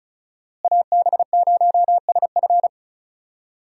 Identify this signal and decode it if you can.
Morse code, heard as a keyed tone, about 34 wpm, 700 Hz. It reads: AB0SF
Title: USA Call Signs - 5 Characters - Repeated 3x - 35wpm